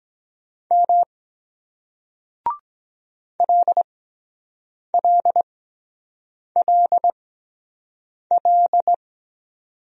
Morse code, heard as a keyed tone, about 22 wpm, spaced as for 7 wpm, 700 Hz. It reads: M LLLL